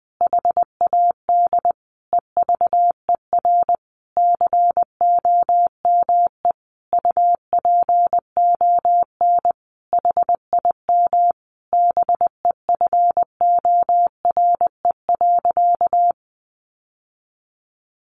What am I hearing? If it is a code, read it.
Morse code, heard as a keyed tone, about 20 wpm, 700 Hz. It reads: HAD EVER COME UPON HIM BEFORE.
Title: Sentences from Top 1100 Words 20wpm